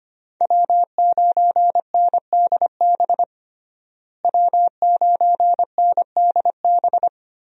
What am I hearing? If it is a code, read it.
Morse code, heard as a keyed tone, about 25 wpm, 700 Hz. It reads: W9NDB W9NDB
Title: USA Call Signs - 5 Characters - Repeated 3x - 25wpm